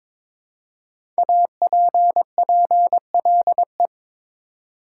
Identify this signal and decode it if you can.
Morse code, heard as a keyed tone, about 23 wpm, 700 Hz. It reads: APPLE
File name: 5 Letter Words - First 3 Letters Given - 22wpm